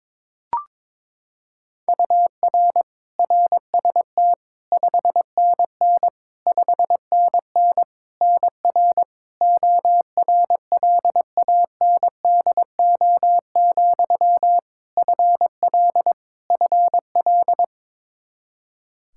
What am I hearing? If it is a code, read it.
Morse code, heard as a keyed tone, about 22 wpm, 700 Hz. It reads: UR RST 5NN 5NN NR ORLANDO, FL FL
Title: Practice QSO - Exchange 1 - Beginning - 22wpm